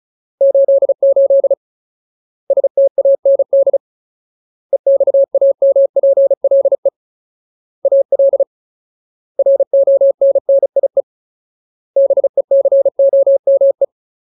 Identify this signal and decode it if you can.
Morse code, heard as a keyed tone, about 35 wpm, 550 Hz. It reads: 88 STAND EXAMPLE AL RONNIE BECOME
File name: Ultimate Word List / 35WPM Ultimate